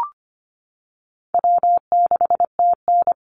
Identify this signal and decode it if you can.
Morse code, heard as a keyed tone, about 25 wpm, 700 Hz. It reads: W6TN
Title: Easy Call Signs - 25wpm